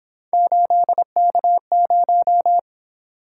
Morse code, heard as a keyed tone, about 26 wpm, 700 Hz. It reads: 8K0
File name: Speed-Racing - Three Letter-Number Combinations - 22wpm